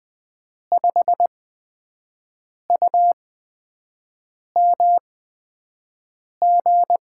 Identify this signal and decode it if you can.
Morse code, heard as a keyed tone, about 20 wpm, spaced as for 5 wpm, 700 Hz. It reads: 5UMG
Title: USA Call Signs - 6 Characters - 8x Character Spacing - 20wpm